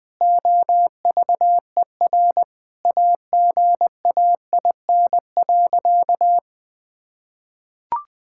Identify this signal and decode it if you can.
Morse code, heard as a keyed tone, about 20 wpm, 700 Hz. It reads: OVER AGAIN.
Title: Sentences with 6-Word Phrases - 20wpm